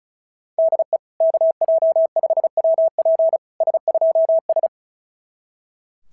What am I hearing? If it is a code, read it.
Morse code, heard as a keyed tone, about 36 wpm, 660 Hz. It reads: DE KJ5WP S2S
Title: SOTA QSO Practice 35wpm